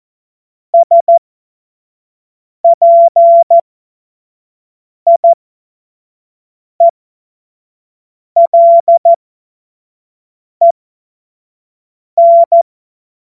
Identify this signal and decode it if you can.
Morse code, heard as a keyed tone, about 14 wpm, spaced as for 5 wpm, 680 Hz. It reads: SPIELEN
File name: Morsekurs / Witze / Witz 02. Februar 2021, 25 F70